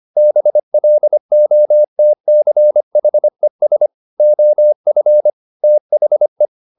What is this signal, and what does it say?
Morse code, heard as a keyed tone, about 25 wpm, 600 Hz. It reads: BLOTCHES OF THE